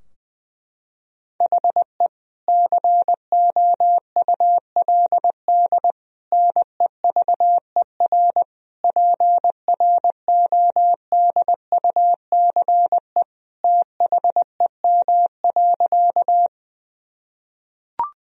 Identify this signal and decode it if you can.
Morse code, heard as a keyed tone, about 21 wpm, 700 Hz. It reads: HE COULD NEVER PRODUCE THEM.
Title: Sentences from Top 1000 Words - Review Part 3 - 20wpm